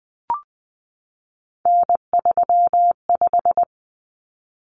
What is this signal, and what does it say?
Morse code, heard as a keyed tone, about 20 wpm, 700 Hz. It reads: N35
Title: Common Cut Numbers - 3-Digit 20wpm